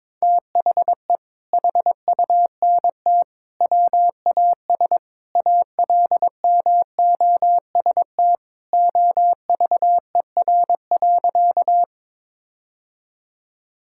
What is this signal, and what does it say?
Morse code, heard as a keyed tone, about 22 wpm, 700 Hz. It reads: THE HUNT WAS ALMOST OVER.
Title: Sentences from Top 1600 Words 22wpm